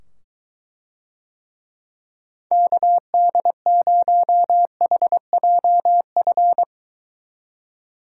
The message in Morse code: KD0HJF

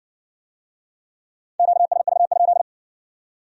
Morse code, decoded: QSV?

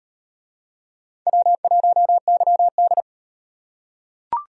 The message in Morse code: W1YD

